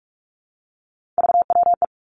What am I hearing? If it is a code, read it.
Morse code, heard as a keyed tone, about 46 wpm, 700 Hz. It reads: VWE
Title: Three Challenging Letter-Number Combinations 45wpm